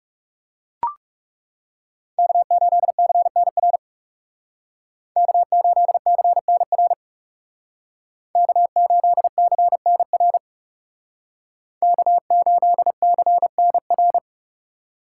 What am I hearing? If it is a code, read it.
Morse code, heard as a keyed tone, about 36 wpm, 700 Hz. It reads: K8CNR K8CNR K8CNR K8CNR
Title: Speed Racing - USA 1x3 Call Signs 30wpm